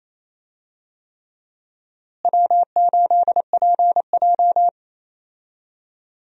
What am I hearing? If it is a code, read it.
Morse code, heard as a keyed tone, about 28 wpm, 700 Hz. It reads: W8PJ
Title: USA Call Signs - 4 Characters - Repeated 3x - 28wpm